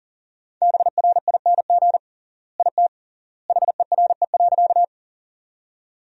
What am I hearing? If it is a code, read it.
Morse code, heard as a keyed tone, about 40 wpm, 700 Hz. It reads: BRING IT HERE.
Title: 3x Word-Spacing - Sentences from Top 500 Words - 40wpm